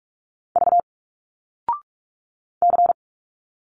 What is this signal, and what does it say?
Morse code, heard as a keyed tone, about 46 wpm, 700 Hz. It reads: V C
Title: Single Letters 45wpm